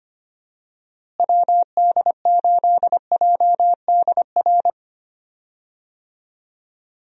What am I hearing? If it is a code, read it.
Morse code, heard as a keyed tone, about 25 wpm, 700 Hz. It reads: WD8JDR
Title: USA Call Signs - 6 Characters - Repeated 3x - 25wpm